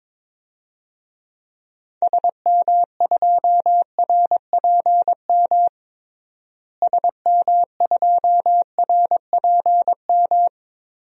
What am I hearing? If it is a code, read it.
Morse code, heard as a keyed tone, about 22 wpm, 700 Hz. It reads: SM2RPM SM2RPM
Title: DXCC-100 Call Signs - 6 Characters - Repeated 3x - 22wpm